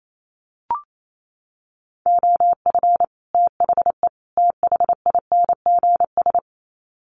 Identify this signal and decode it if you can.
Morse code, heard as a keyed tone, about 28 wpm, 700 Hz. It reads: OF THE THINGS
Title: Sets of 3 Words 28wpm